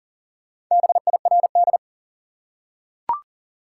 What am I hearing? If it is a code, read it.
Morse code, heard as a keyed tone, about 40 wpm, 700 Hz. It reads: BIRD
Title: Four Letter Words - 40wpm